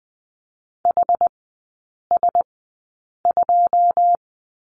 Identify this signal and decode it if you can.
Morse code, heard as a keyed tone, about 20 wpm, spaced as for 8 wpm, 690 Hz. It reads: HS2
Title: 081 Lesson 14e - Number 2 - Three Characters 20wpm